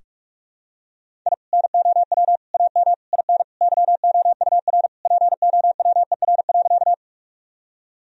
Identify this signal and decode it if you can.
Morse code, heard as a keyed tone, about 41 wpm, 700 Hz. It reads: I NOW AM IN YOUR POWER.
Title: Sentences from Top 1000 Words - Review Part 2 - 45wpm